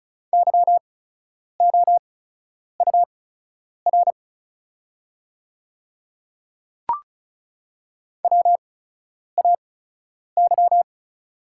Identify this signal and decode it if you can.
Morse code, heard as a keyed tone, about 36 wpm, spaced as for 10 wpm, 700 Hz. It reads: YOUR WAY
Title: 8x Character Spacing - Top 100 Words - 35wpm